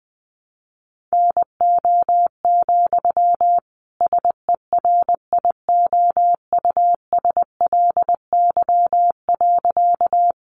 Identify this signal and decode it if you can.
Morse code, heard as a keyed tone, about 20 wpm, 700 Hz. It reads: NO, SERIOUSLY.
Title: Sentences from Top 1200 Words 20wpm